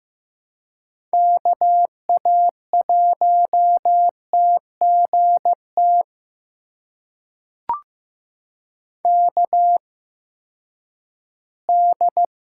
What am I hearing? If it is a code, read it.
Morse code, heard as a keyed tone, about 15 wpm, spaced as for 10 wpm, 700 Hz. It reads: KA1TGT KD
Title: USA Call Signs - 6 Characters - 8x Character Spacing - 15wpm